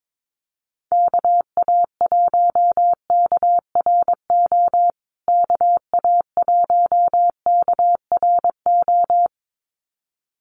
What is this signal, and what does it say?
Morse code, heard as a keyed tone, about 22 wpm, 700 Hz. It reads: KA1KRO KA1KRO
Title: USA Call Signs Repeated 2x 22wpm